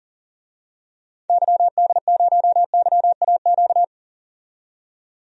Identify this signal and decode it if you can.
Morse code, heard as a keyed tone, about 40 wpm, 700 Hz. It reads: YD0YAQ
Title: DXCC-100 Call Signs - 6 Characters - 40wpm